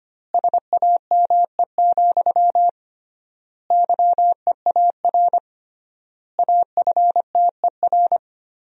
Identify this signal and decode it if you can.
Morse code, heard as a keyed tone, about 25 wpm, 700 Hz. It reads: SAME, YEAR AFTER
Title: 3x Word-Spacing - Sentences from Top 300 Words - 25wpm